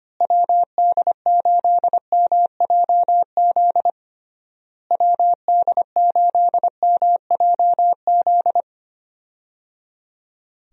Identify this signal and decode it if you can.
Morse code, heard as a keyed tone, about 25 wpm, 700 Hz. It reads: WD8MJZ WD8MJZ
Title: USA Call Signs - 6 Characters - Repeated 3x - 25wpm